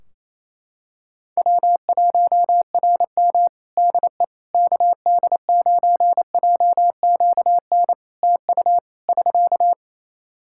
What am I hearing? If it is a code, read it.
Morse code, heard as a keyed tone, about 28 wpm, 700 Hz. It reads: W1RM DE KD9JQN TU <SK>
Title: K1USN SST QSO Practice 28wpm